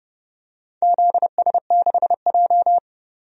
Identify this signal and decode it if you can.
Morse code, heard as a keyed tone, about 29 wpm, 700 Hz. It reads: ZS6J
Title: DXCC-100 Call Signs - 4 Characters - 30wpm